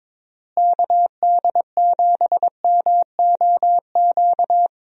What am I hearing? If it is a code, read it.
Morse code, heard as a keyed tone, about 22 wpm, 700 Hz. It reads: KD7MOQ
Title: USA Call Signs - 6 Characters - Repeated 3x - 22wpm